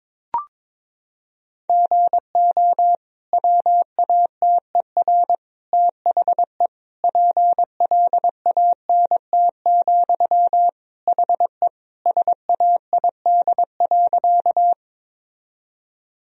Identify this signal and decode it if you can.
Morse code, heard as a keyed tone, about 22 wpm, 700 Hz. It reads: GO WATER THE PLANT, HE SAID.
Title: Sentences from Top 1000 Words - Review Part 1 - 22wpm